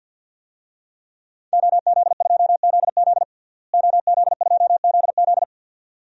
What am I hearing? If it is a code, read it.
Morse code, heard as a keyed tone, about 52 wpm, 700 Hz. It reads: OZ2ZZ OZ2ZZ
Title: DXCC-100 Call Signs - 5 Characters - Repeated 2x - 50wpm